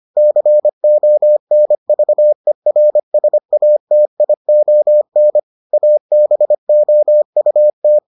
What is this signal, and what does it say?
Morse code, heard as a keyed tone, about 25 wpm, 600 Hz. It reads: CONVERSATION ABOUT